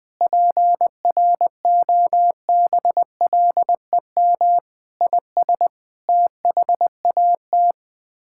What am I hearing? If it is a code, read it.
Morse code, heard as a keyed tone, about 20 wpm, 700 Hz. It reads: PROBLEM IS THAT
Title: Sets of 6 Words - 20wpm